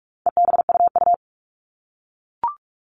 Morse code, heard as a keyed tone, about 44 wpm, 700 Hz. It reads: EBUU
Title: All Cut Numbers - 4-Digit 45wpm